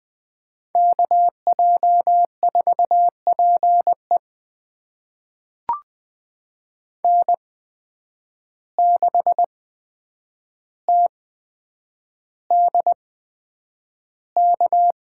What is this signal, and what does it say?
Morse code, heard as a keyed tone, about 20 wpm, spaced as for 9 wpm, 700 Hz. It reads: KJ4PE N6TDK
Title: USA Call Signs - 5 Characters - 8x Character Spacing - 20wpm